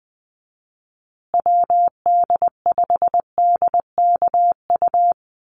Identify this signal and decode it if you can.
Morse code, heard as a keyed tone, about 20 wpm, 700 Hz. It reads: WD5DKU
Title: USA Call Signs Repeated 3x 20wpm